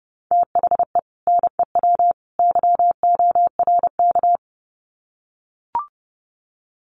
Morse code, heard as a keyed tone, about 30 wpm, 700 Hz. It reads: THE NEW YORK
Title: Sets of 3 Words 30wpm